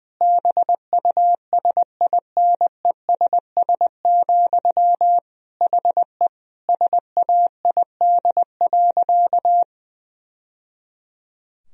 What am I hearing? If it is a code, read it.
Morse code, heard as a keyed tone, about 20 wpm, 700 Hz. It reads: BUSINESS, HE SAID.
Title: Sentences from Top 1000 Words - Review Part 3 - 20wpm